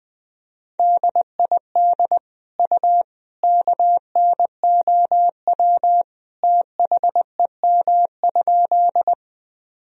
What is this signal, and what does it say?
Morse code, heard as a keyed tone, about 21 wpm, 700 Hz. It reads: DID U KNOW THEM?
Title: Sentences from Top 100 Words and Abbreviations 20wpm